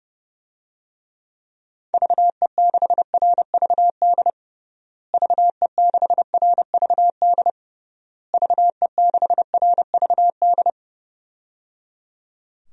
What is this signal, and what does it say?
Morse code, heard as a keyed tone, about 30 wpm, 700 Hz. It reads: VE6RVD VE6RVD VE6RVD
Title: DXCC-100 Call Signs - 6 Characters - Repeated 3x - 30wpm